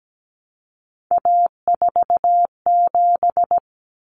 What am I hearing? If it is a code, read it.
Morse code, heard as a keyed tone, about 17 wpm, 700 Hz. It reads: A47
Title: Common Cut Numbers - 3-Digit 17wpm